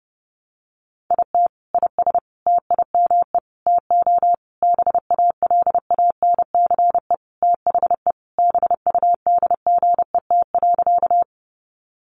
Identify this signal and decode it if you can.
Morse code, heard as a keyed tone, about 30 wpm, 700 Hz. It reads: IT IS TIME TO BALANCE THE BUDGET.
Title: Sentences from Top 1200 Words 30wpm